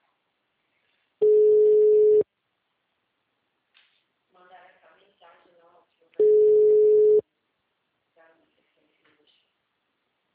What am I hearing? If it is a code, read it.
Ringback tone (Europe, 425 Hz), with speech.